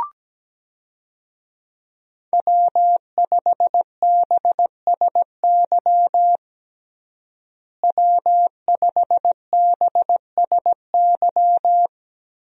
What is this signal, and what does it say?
Morse code, heard as a keyed tone, about 17 wpm, 700 Hz. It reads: W5BSY W5BSY
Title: USA Call Signs - 5 Characters - Repeated 2x - 17wpm